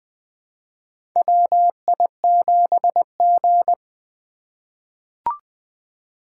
Morse code, heard as a keyed tone, about 20 wpm, 700 Hz. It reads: WI7G
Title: USA Call Signs - 4 Characters - 2x Character Spacing - 20wpm